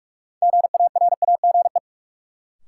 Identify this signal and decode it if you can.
Morse code, heard as a keyed tone, about 41 wpm, 700 Hz. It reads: GARAGE
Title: Sentences from Top 1600 Words - New Word Drill Alt - 45wpm